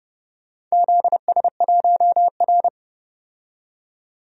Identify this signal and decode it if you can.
Morse code, heard as a keyed tone, about 31 wpm, 700 Hz. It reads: ZS1R
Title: DXCC-100 Call Signs - 4 Characters - 30wpm